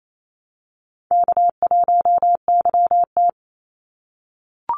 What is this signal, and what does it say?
Morse code, heard as a keyed tone, about 28 wpm, 700 Hz. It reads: K1YT
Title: Easy Call Signs - 28wpm